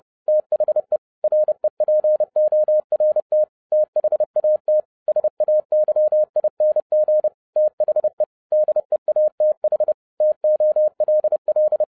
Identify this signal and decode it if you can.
Morse code, heard as a keyed tone, about 30 wpm, 610 Hz. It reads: THE REPORT THAT SAYING THE DEATH TOLL